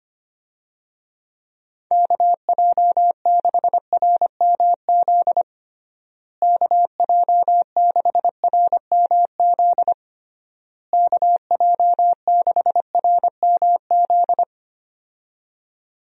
Morse code, heard as a keyed tone, about 25 wpm, 700 Hz. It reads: KJ6RMZ KJ6RMZ KJ6RMZ
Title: USA Call Signs - 6 Characters - Repeated 3x - 25wpm